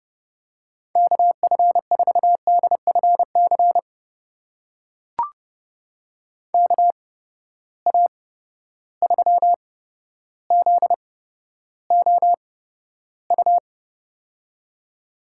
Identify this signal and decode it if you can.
Morse code, heard as a keyed tone, about 30 wpm, spaced as for 13 wpm, 700 Hz. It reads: KF4DFC KA3ZOU
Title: USA Call Signs - 6 Characters - 8x Character Spacing - 30wpm